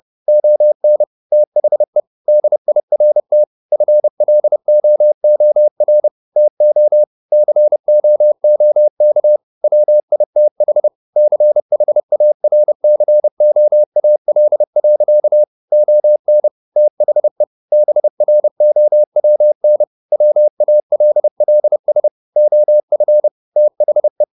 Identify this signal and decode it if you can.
Morse code, heard as a keyed tone, about 30 wpm, 600 Hz. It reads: ON THE DIRT FLOOR TO COOK WITH CHARCOAL. ON THE BROWN WALLS OF THE
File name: the old man and the sea / the old man and the sea 1 30wpm